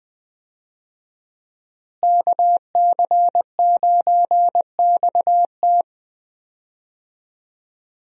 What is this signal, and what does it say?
Morse code, heard as a keyed tone, about 20 wpm, 700 Hz. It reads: KC9XT
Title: USA Call Signs - 5 Characters - Repeated 2x - 20wpm